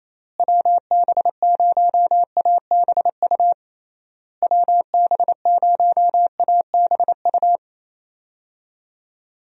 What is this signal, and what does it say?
Morse code, heard as a keyed tone, about 28 wpm, 700 Hz. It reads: WB0ABU WB0ABU
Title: USA Call Signs - 6 Characters - Repeated 3x - 28wpm